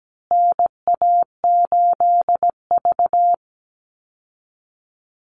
Morse code, heard as a keyed tone, about 17 wpm, 700 Hz. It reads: NA8V
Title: Call Signs Repeated 3x - 17wpm